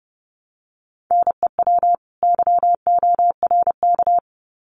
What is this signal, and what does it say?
Morse code, heard as a keyed tone, about 30 wpm, 700 Hz. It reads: NEW YORK
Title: Sets of 2 Words 30wpm